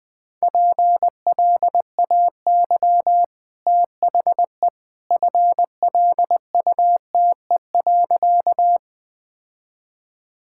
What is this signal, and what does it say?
Morse code, heard as a keyed tone, about 20 wpm, 700 Hz. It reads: PLAY THE FLUTE.
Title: Easy Sentences with Hints - 20wpm